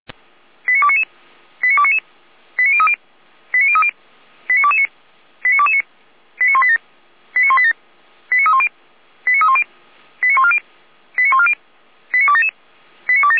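Five-tone selective call (ZVEI), heard as quick runs of five tones, tones of about 70 ms. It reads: 89299 89299 89030 89030 89229 89229 89187 89187 89310 89310 89140 89140 89277 89277
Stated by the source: Received on an SDR; ZVEI five-tone code 89165.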